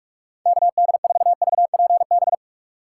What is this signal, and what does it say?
Morse code, heard as a keyed tone, about 59 wpm, spaced as for 52 wpm, 700 Hz. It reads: KD4VPB
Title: Speed Racing - USA 2x3 Call Signs 45wpm